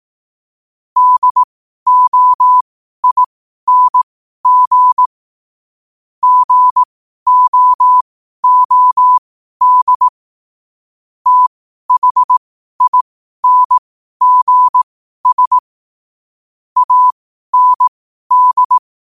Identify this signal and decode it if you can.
Morse code, heard as a keyed tone, about 18 wpm, spaced as for 12 wpm, 1000 Hz. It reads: DOING GOOD THINGS AND
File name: qotdfiles / 12wpm